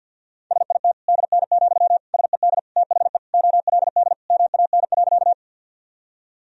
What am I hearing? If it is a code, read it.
Morse code, heard as a keyed tone, about 52 wpm, 700 Hz. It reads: SIT DN, SED THE OLD MAN.
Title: Sentences from Top 500 Words and Abbreviations 50wpm